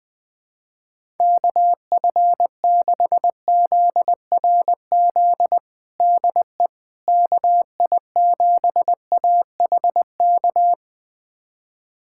Morse code, heard as a keyed tone, about 20 wpm, 700 Hz. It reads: KF6ZRZ DE KI7AHK
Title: USA Callsign DE Callsign 20wpm